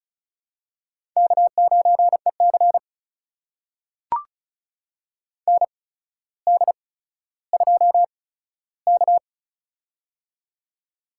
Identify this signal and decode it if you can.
Morse code, heard as a keyed tone, about 36 wpm, spaced as for 15 wpm, 700 Hz. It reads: K9EC ND2K
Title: USA Call Signs - 8x Character Spacing - 35wpm